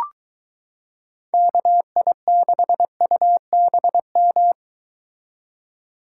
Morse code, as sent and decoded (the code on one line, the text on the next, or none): -.- .. -.... ..- -... --
KI6UBM